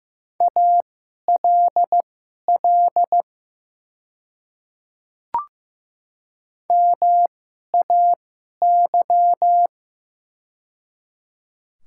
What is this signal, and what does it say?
Morse code, heard as a keyed tone, about 15 wpm, spaced as for 11 wpm, 690 Hz. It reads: ALL MAY
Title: 2x Character Spacing - Top 300 Words - 15wpm